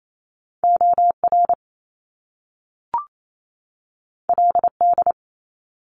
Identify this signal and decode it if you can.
Morse code, heard as a keyed tone, about 28 wpm, 700 Hz. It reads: OR LD
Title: Top Two Letter Combinations in Words 28wpm